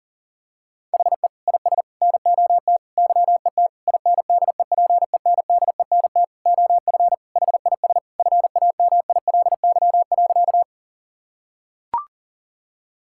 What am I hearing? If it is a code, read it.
Morse code, heard as a keyed tone, about 40 wpm, 700 Hz. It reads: HE IS NOT YET INDEPENDENT OF HIS FAMILY.
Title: Sentences from Top 1000 Words - Review Part 6 - 40wpm